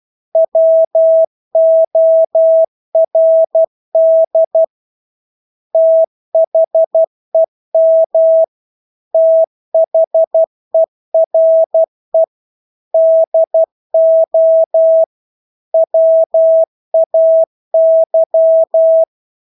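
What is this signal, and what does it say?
Morse code, heard as a keyed tone, about 12 wpm, 650 Hz. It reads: WORD THEM THERE DO WAY
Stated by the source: Morse practice of English words at 12 wpm.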